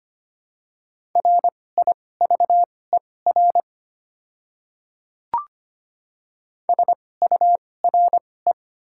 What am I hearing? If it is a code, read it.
Morse code, heard as a keyed tone, about 25 wpm, spaced as for 18 wpm, 700 Hz. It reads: RIVER SURE
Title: 2x Character Spacing - Top 300 Words - 25wpm